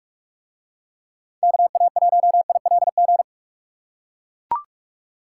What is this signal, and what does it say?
Morse code, heard as a keyed tone, about 44 wpm, 700 Hz. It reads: KA1ILG